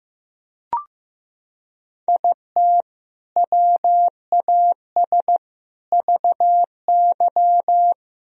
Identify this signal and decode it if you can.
Morse code, heard as a keyed tone, about 15 wpm, 700 Hz. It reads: IT WAS VY
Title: Sentences from Top 300 Words and Abbreviations 15wpm